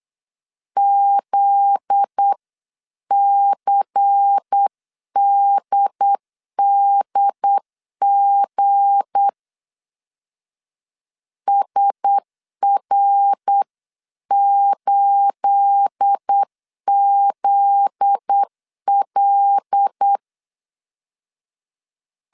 Morse code, heard as a keyed tone, about 8 wpm, 800 Hz. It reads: ZCDDG SR8ZL
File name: CW / DET / 1 9 1500C S B 8wpm 5624